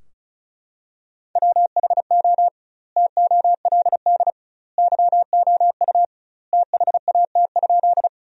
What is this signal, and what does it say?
Morse code, heard as a keyed tone, about 34 wpm, 700 Hz. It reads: WHO TOLD YOU THAT?
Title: 2x Word-Spacing - Sentences from Top 400 Words - 35wpm